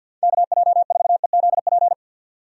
Morse code, heard as a keyed tone, about 51 wpm, 700 Hz. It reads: KJ4EZP
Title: USA Call Signs - 6 Characters - 50wpm